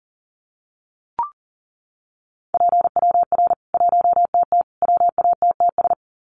Morse code, heard as a keyed tone, about 40 wpm, 700 Hz. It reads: PWR 1TT WATTS
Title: Common QSO Phrases 40wpm